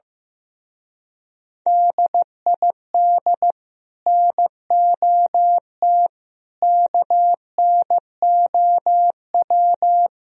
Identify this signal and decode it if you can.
Morse code, heard as a keyed tone, about 15 wpm, 690 Hz. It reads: DID NOT KNOW
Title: Speed-Racing - Sets of 5 Words 15wpm